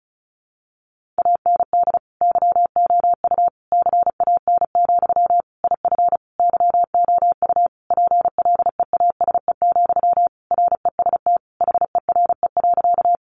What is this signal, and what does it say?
Morse code, heard as a keyed tone, about 34 wpm, 700 Hz. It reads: AND YOU CAN, IF YOU PLEASE, REST HERE.
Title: Sentences from Top 1500 Words 35wpm